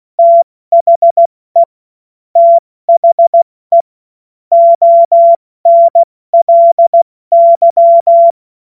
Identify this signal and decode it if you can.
Morse code, heard as a keyed tone, about 16 wpm, 680 Hz. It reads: THE THE ONLY
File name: Morsekurs / Witze / Witz 27. September 2022 70 F80